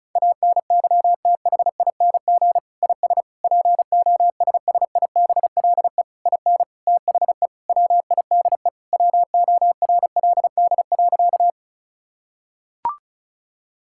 Morse code, heard as a keyed tone, about 35 wpm, 700 Hz. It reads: ANYTHING IS POSSIBLE IN THE WIDE WORLD.
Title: Sentences from Top 1000 Words - Review Part 6 - 35wpm